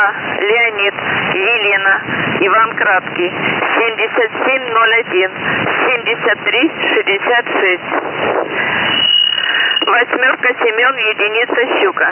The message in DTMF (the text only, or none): none